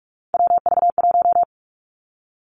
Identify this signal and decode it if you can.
Morse code, heard as a keyed tone, about 46 wpm, 700 Hz. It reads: WV1
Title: Three Challenging Letter-Number Combinations 45wpm